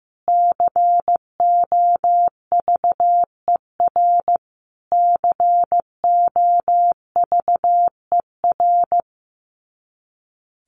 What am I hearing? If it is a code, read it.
Morse code, heard as a keyed tone, about 14 wpm, 700 Hz. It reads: COVER COVER
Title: Top 500 Words Repeated 15wpm